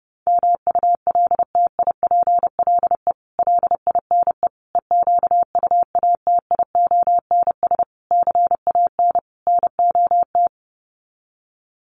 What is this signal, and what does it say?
Morse code, heard as a keyed tone, about 30 wpm, 700 Hz. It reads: MULTIPLE LINE EQUATIONS CAN NOT
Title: Sets of 5 Words 30wpm